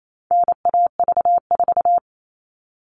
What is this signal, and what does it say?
Morse code, heard as a keyed tone, about 28 wpm, 700 Hz. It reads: NAV4
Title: All Cut Numbers - 4-Digit 28wpm